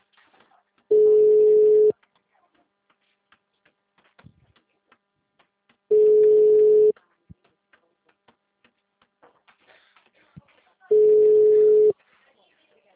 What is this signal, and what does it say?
Ringback tone (Europe, 425 Hz).